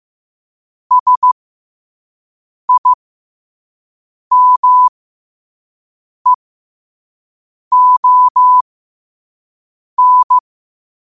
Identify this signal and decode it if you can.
Morse code, heard as a keyed tone, about 15 wpm, spaced as for 5 wpm, 1000 Hz. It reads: SIMEON
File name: qotdfiles / 5wpm